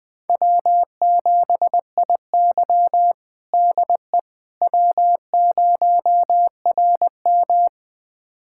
Morse code, heard as a keyed tone, about 20 wpm, 700 Hz. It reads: W7IY DE W0RM
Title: USA Callsign DE Callsign - 4 Characters - 20wpm